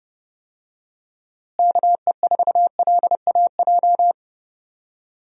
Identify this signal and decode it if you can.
Morse code, heard as a keyed tone, about 30 wpm, 700 Hz. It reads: KE4LAJ
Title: USA Call Signs - 6 Characters - Repeated 3x - 30wpm